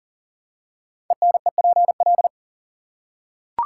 Morse code, heard as a keyed tone, about 38 wpm, 700 Hz. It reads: ENEPL
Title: Code Groups - 5 Letters - 40wpm